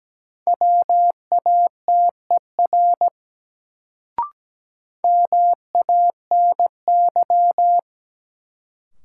Morse code, heard as a keyed tone, about 17 wpm, 700 Hz. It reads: WATER MANY
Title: Sentences from Top 100 Words - New Word Drill - 17wpm